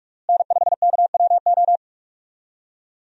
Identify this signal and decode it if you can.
Morse code, heard as a keyed tone, about 42 wpm, 700 Hz. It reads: N5KWO
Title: USA Call Signs - 5 Characters - Repeated 3x - 45wpm